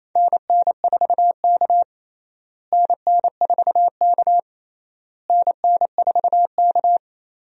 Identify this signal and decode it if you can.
Morse code, heard as a keyed tone, about 28 wpm, 700 Hz. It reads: NN4K NN4K NN4K
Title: USA Call Signs - 4 Characters - Repeated 3x - 28wpm